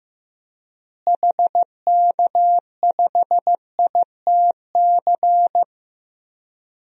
Morse code, heard as a keyed tone, about 15 wpm, 700 Hz. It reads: HK5ITC
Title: DXCC-100 Call Signs - 6 Characters - Repeated 2x - 15wpm